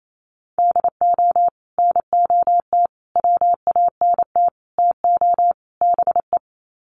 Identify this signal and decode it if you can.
Morse code, heard as a keyed tone, about 28 wpm, 700 Hz. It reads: DO NOT WANT TO BE
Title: Sets of 5 Words 28wpm